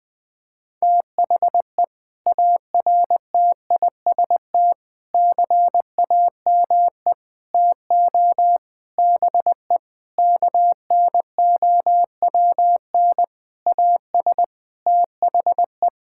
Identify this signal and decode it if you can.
Morse code, heard as a keyed tone, about 21 wpm, 700 Hz. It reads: THE ARTIST CAME TO BE KNOWN AS THE
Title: Sentences with 6-Word Phrases - 20wpm